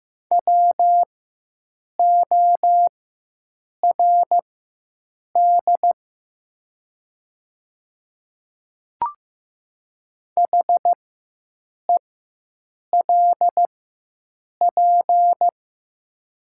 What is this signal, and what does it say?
Morse code, heard as a keyed tone, about 15 wpm, spaced as for 7 wpm, 700 Hz. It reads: WORD HELP
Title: 4x Character Spacing - Top 200 Words - 15wpm